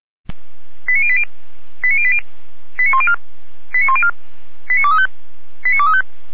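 Five-tone selective call (ZVEI), heard as quick runs of five tones, tones of about 70 ms. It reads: 89080 89080 89104 89104 89235 89235
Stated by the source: Five-tone ZVEI selective call 89120, received on SDR.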